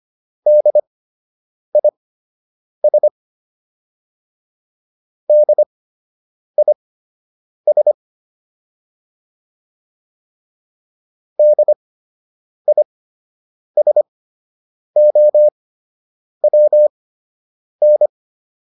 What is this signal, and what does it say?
Morse code, heard as a keyed tone, about 25 wpm, spaced as for 8 wpm, 600 Hz. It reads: DIS DIS DISOWN
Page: fundamental-practice-files